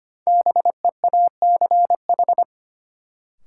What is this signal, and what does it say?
Morse code, heard as a keyed tone, about 25 wpm, 700 Hz. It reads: BEACH
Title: Sentences from Top 1500 Words - New Word Drill Alt - 25wpm